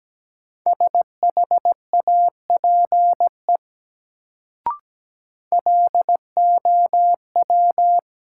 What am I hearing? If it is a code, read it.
Morse code, heard as a keyed tone, about 17 wpm, 700 Hz. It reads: SHAPE LOW
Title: Sentences from Top 500 Words - New Word Drill - 17wpm